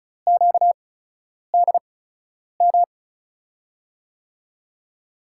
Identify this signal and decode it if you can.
Morse code, heard as a keyed tone, about 36 wpm, 700 Hz. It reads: Q D M
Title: USA Call Signs - 6 Characters - 8x Character Spacing - 35wpm